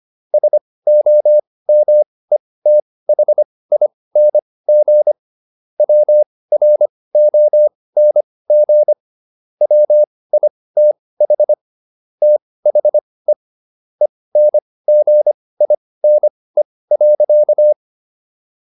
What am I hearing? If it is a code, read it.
Morse code, heard as a keyed tone, about 25 wpm, spaced as for 18 wpm, 600 Hz. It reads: SOMETHING WRONG WITH THE ENGINE.